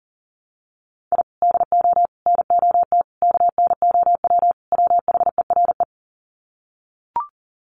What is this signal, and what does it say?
Morse code, heard as a keyed tone, about 37 wpm, 700 Hz. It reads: I DO NOT KNOW WHERE
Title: Sets of 5 Words 40wpm